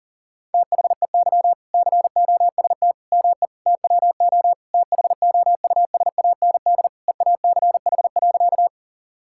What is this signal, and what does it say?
Morse code, heard as a keyed tone, about 40 wpm, 700 Hz. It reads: THEY COST ME TWO THOUSAND EACH.
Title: Sentences from Top 1000 Words - Review Part 5 - 40wpm